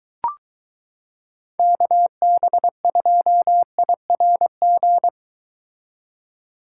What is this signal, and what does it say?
Morse code, heard as a keyed tone, about 23 wpm, 700 Hz. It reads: KB2IRG